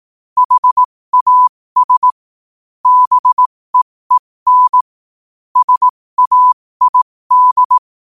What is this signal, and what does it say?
Morse code, heard as a keyed tone, about 18 wpm, spaced as for 15 wpm, 1000 Hz. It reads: HAS BEEN SAID
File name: qotdfiles / 15wpm